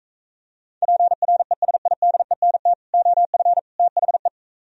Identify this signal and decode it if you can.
Morse code, heard as a keyed tone, about 45 wpm, 700 Hz. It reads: PRESIDENT OF THE